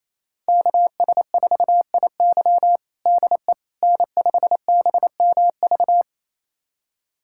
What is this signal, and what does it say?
Morse code, heard as a keyed tone, about 28 wpm, 700 Hz. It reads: KS4IY DE N5BMV
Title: USA Callsign DE Callsign - 5 Characters - 28wpm